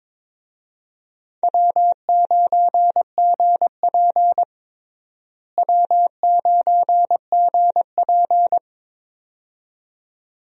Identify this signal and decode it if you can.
Morse code, heard as a keyed tone, about 22 wpm, 700 Hz. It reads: W9GP W9GP
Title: USA Call Signs - 4 Characters - Repeated 2x - 22wpm